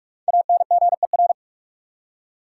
Morse code, heard as a keyed tone, about 46 wpm, 700 Hz. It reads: ANGER
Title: Five Letter Words - 45wpm